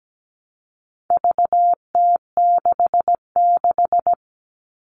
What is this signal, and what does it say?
Morse code, heard as a keyed tone, about 17 wpm, 700 Hz. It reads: VT66
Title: All Cut Numbers - 4-Digit 17wpm